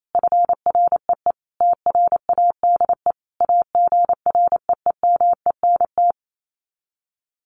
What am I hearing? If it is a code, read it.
Morse code, heard as a keyed tone, about 28 wpm, 700 Hz. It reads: FREE TRADE AGREEMENT
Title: Sets of 5 Words Encore 28wpm